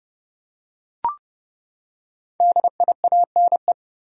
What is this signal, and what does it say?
Morse code, heard as a keyed tone, about 30 wpm, 700 Hz. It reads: DIANE
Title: USA Common Names - 30wpm